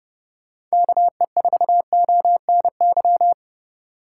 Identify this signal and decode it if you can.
Morse code, heard as a keyed tone, about 37 wpm, 700 Hz. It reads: KE4ONY
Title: Speed Racing - USA 2x3 Call Signs 30wpm